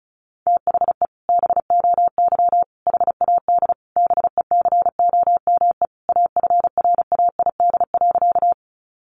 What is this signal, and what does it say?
Morse code, heard as a keyed tone, about 34 wpm, 700 Hz. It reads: THE BOY HAD BECOME AFRAID.
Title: Sentences from Top 1500 Words 35wpm